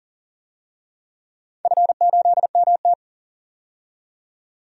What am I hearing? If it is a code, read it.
Morse code, heard as a keyed tone, about 40 wpm, 700 Hz. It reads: F8MT
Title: DXCC-100 Call Signs - 4 Characters - 40wpm